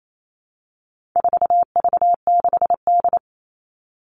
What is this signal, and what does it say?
Morse code, heard as a keyed tone, about 28 wpm, 700 Hz. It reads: 4V6D
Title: All Cut Numbers - 4-Digit 28wpm